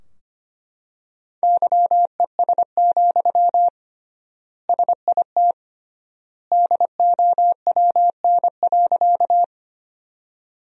Morse code, heard as a keyed tone, about 25 wpm, 700 Hz. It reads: YES, SIT DOWN.
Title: 3x Word-Spacing - Sentences from Top 500 Words - 25wpm